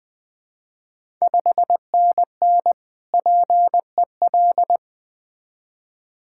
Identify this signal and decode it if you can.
Morse code, heard as a keyed tone, about 20 wpm, 700 Hz. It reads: 5NN PEL
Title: Ontario QSO Party Exchange 20wpm